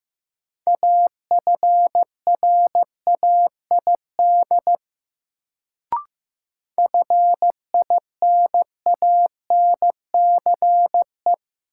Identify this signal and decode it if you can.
Morse code, heard as a keyed tone, about 15 wpm, 700 Hz. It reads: AFRAID FINANCE
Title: Sentences from Top 1500 Words - New Word Drill - 15wpm